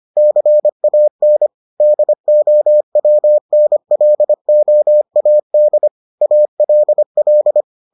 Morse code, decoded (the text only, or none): CAN DOWNLOAD ALL